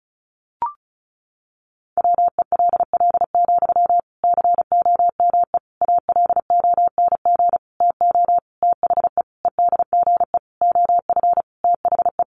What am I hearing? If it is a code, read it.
Morse code, heard as a keyed tone, about 34 wpm, 700 Hz. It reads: WELL, COME ALONG TO THE EDGE OF THE
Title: Sentences from Top 1300 Words 35wpm